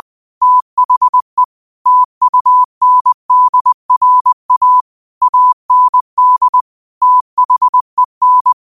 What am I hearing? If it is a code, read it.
Morse code, heard as a keyed tone, about 20 wpm, 1000 Hz. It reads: THE TUNDRA AND THEN
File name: qotdfiles / 20wpm